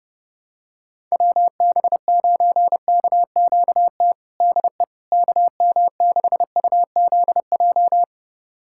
Morse code, heard as a keyed tone, about 30 wpm, 700 Hz. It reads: WB9KQT DE KM6UZJ
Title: USA Callsign DE Callsign - 6 Characters - 30wpm